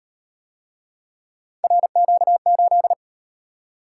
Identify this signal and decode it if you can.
Morse code, heard as a keyed tone, about 37 wpm, 700 Hz. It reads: RQ8